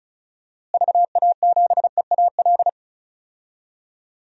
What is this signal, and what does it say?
Morse code, heard as a keyed tone, about 34 wpm, 700 Hz. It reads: VA7EAL
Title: DXCC-100 Call Signs - 6 Characters - Repeated 3x - 35wpm